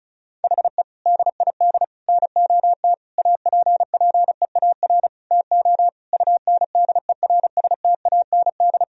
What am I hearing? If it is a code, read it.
Morse code, heard as a keyed tone, about 34 wpm, 700 Hz. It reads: HE DID NOT APPEAR TO UNDERSTAND
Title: Sentences from Top 1000 Words - Review Part 2 - 35wpm